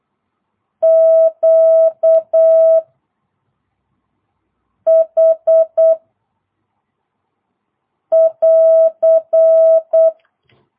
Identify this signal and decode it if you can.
Morse code, heard as a keyed tone, about 8 wpm, 650 Hz. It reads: Q H <AR>